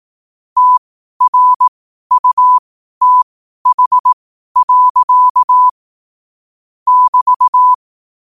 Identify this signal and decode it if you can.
Morse code, heard as a keyed tone, about 18 wpm, spaced as for 12 wpm, 1000 Hz. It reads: TRUTH. <BT>
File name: qotdfiles / 12wpm